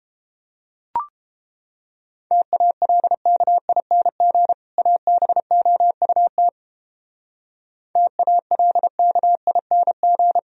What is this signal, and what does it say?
Morse code, heard as a keyed tone, about 31 wpm, 700 Hz. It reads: TALKING ABOUT TALKING